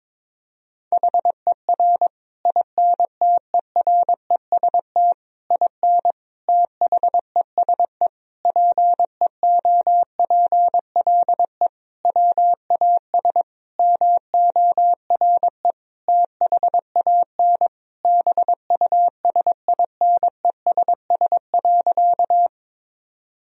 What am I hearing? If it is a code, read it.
Morse code, heard as a keyed tone, about 22 wpm, 700 Hz. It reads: HER INTEREST IN THESE PEOPLE WAS MORE THAN BUSINESS.
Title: Sentences from Top 1000 Words - Review Part 2 - 22wpm